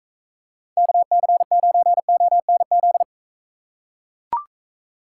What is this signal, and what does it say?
Morse code, heard as a keyed tone, about 40 wpm, 700 Hz. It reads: KC9ONZ